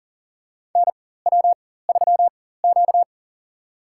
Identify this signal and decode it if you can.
Morse code, heard as a keyed tone, about 38 wpm, spaced as for 18 wpm, 700 Hz. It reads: NW3Q
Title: USA Call Signs - 4 Characters - 4x Character Spacing - 40wpm